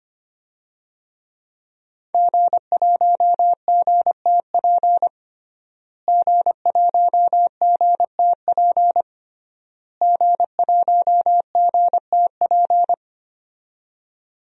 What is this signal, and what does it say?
Morse code, heard as a keyed tone, about 25 wpm, 700 Hz. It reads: G1GTP G1GTP G1GTP
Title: DXCC-100 Call Signs - 5 Characters - Repeated 3x - 25wpm